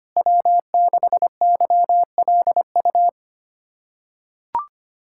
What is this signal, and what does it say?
Morse code, heard as a keyed tone, about 25 wpm, 700 Hz. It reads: W6YLU
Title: USA Callsign DE Callsign 25wpm